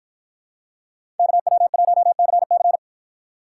Morse code, heard as a keyed tone, about 48 wpm, 700 Hz. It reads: KW1CC